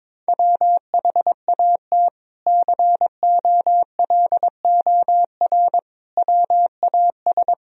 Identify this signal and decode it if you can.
Morse code, heard as a keyed tone, about 22 wpm, 700 Hz. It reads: WHAT COLOR WAS
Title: Sentences from Top 1000 Words - Review Part 2 - 22wpm